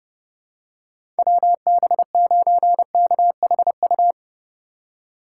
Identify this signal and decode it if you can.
Morse code, heard as a keyed tone, about 30 wpm, 700 Hz. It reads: WB9KHU
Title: USA Call Signs - 6 Characters - 3x Character Spacing - 30wpm